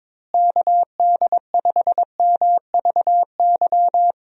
Morse code, heard as a keyed tone, about 21 wpm, 700 Hz. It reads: KD5MVY
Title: USA Call Signs - 6 Characters - Repeated 2x - 22wpm